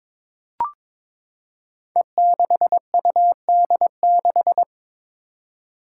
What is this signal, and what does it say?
Morse code, heard as a keyed tone, about 22 wpm, 700 Hz. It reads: E6UD6
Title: Five Challenging Letter-Number Combinations 22wpm